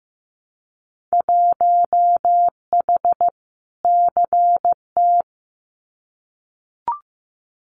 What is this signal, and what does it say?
Morse code, heard as a keyed tone, about 15 wpm, 700 Hz. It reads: 1H CT
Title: Winter Field Day Exchanges 15wpm